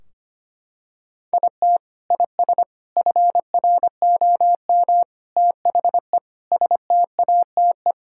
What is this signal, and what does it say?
Morse code, heard as a keyed tone, about 25 wpm, 700 Hz. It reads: IT IS FROM THE STATE
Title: Sentences from Top 1600 Words 25wpm